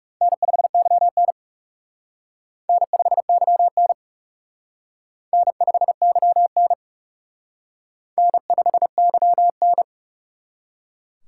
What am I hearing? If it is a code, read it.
Morse code, heard as a keyed tone, about 36 wpm, 700 Hz. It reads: N5YN N5YN N5YN N5YN
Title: Speed Racing - USA 1x2 Call Signs 30wpm